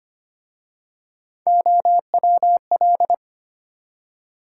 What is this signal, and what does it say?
Morse code, heard as a keyed tone, about 25 wpm, 700 Hz. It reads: OWL
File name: practice / 25wpm / Easy Words with Hints - 25wpm